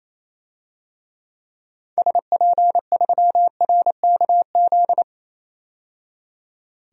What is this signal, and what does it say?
Morse code, heard as a keyed tone, about 28 wpm, 700 Hz. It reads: SP3RKZ
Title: DXCC-100 Call Signs - 6 Characters - 28wpm